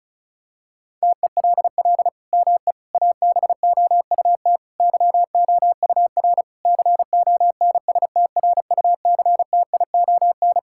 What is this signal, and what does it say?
Morse code, heard as a keyed tone, about 36 wpm, 700 Hz. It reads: TELL ME ABOUT YOUR CONSTRUCTION
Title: Sentences from Top 1600 Words 35wpm